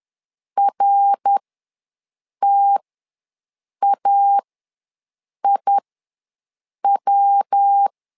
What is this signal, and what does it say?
Morse code, heard as a keyed tone, about 10 wpm, spaced as for 6 wpm, 800 Hz. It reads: RTAIW
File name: LEZIONI / 3 MWRO / 3 MWR ws 10wpm 2151